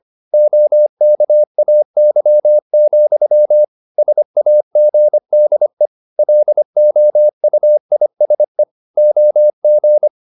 Morse code, heard as a keyed tone, about 25 wpm, 600 Hz. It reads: OKAY, SAGDE LOUISE OG
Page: Ferie 25 wpm